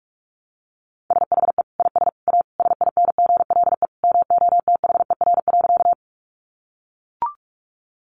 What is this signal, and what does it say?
Morse code, heard as a keyed tone, about 44 wpm, 700 Hz. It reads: SHE IS A SINGLE MOTHER.
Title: Sentences from Top 700 Words 45wpm